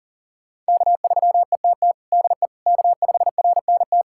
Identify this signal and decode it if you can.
Morse code, heard as a keyed tone, about 40 wpm, 700 Hz. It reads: K3ETT DE K5RNT
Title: USA Callsign DE Callsign - 5 Characters - 40wpm